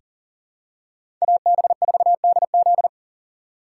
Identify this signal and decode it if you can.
Morse code, heard as a keyed tone, about 40 wpm, 700 Hz. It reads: AB4DZ
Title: USA Call Signs - 5 Characters - 4x Character Spacing - 40wpm